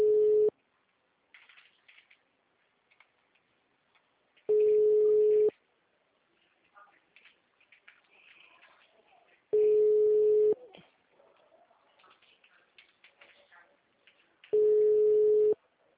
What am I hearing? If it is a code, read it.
Ringback tone (Europe, 425 Hz).
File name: out- -2033- - - .542